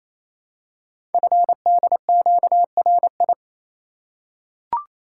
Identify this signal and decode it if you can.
Morse code, heard as a keyed tone, about 28 wpm, 700 Hz. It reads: FDQRI
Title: Code Groups - 5 Letters - 28wpm